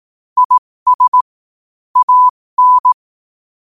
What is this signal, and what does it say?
Morse code, heard as a keyed tone, about 18 wpm, spaced as for 15 wpm, 1000 Hz. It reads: IS AN
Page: New quotes every day in morse code at 15 Words per minute.